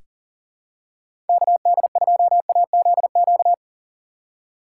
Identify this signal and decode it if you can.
Morse code, heard as a keyed tone, about 40 wpm, 700 Hz. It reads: KD2AZQ
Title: USA Call Signs - 6 Characters - 2x Character Spacing - 40wpm